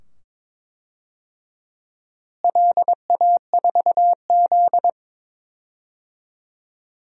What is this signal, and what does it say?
Morse code, heard as a keyed tone, about 22 wpm, 700 Hz. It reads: LA4Z
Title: DXCC-100 Call Signs - 4 Characters - Repeated 2x - 22wpm